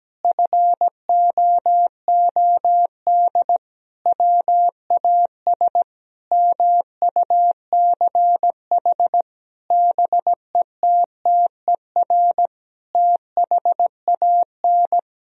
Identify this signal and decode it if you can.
Morse code, heard as a keyed tone, about 17 wpm, 700 Hz. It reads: FOOD WAS MUCH BETTER THAN
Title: Sentences from Top 1000 Words - Review Part 2 - 17wpm